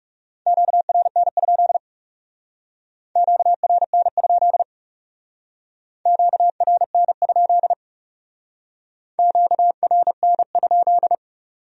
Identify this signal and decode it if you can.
Morse code, heard as a keyed tone, about 37 wpm, 700 Hz. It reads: QRN? QRN? QRN? QRN?
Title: Speed-Racing Q-Codes - 30wpm